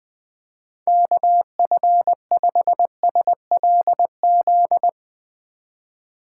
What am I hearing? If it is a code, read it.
Morse code, heard as a keyed tone, about 20 wpm, 700 Hz. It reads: KF5SLZ